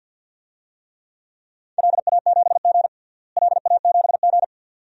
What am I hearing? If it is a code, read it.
Morse code, heard as a keyed tone, about 52 wpm, 700 Hz. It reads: LA7G LA7G
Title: DXCC-100 Call Signs - 4 Characters - Repeated 2x - 50wpm